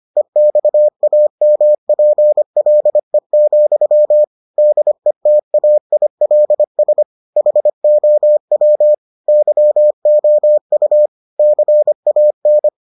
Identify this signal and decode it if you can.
Morse code, heard as a keyed tone, about 25 wpm, 600 Hz. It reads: EXAMPLE, DETAILS HOW YOU CAN